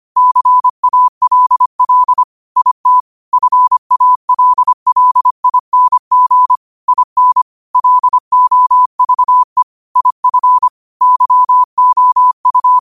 Morse code, heard as a keyed tone, about 25 wpm, 1000 Hz. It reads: CALL IT FALLING IN LOVE IF YOU